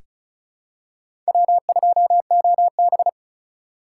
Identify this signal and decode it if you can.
Morse code, heard as a keyed tone, about 34 wpm, 700 Hz. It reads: W2OB
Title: USA Call Signs - 4 Characters - 2x Character Spacing - 35wpm